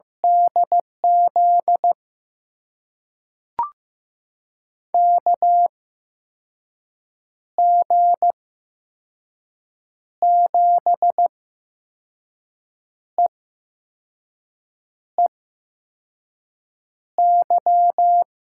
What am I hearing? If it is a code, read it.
Morse code, heard as a keyed tone, about 15 wpm, spaced as for 5 wpm, 700 Hz. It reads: DZ KG7EEY
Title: USA Call Signs - 6 Characters - 8x Character Spacing - 15wpm